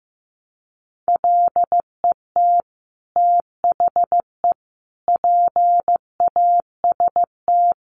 Morse code, heard as a keyed tone, about 15 wpm, 700 Hz. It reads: LET THE PAST
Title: Sentences from Top 1200 Words 15wpm